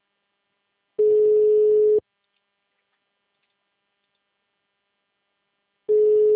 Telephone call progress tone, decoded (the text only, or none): ringback tone